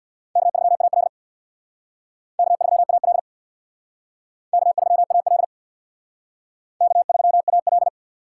Morse code, heard as a keyed tone, about 68 wpm, 690 Hz. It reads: K3AL K3AL K3AL K3AL